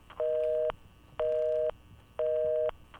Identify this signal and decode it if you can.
Busy tone (North America), 480 + 620 Hz, with noise.
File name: s busy tone bt33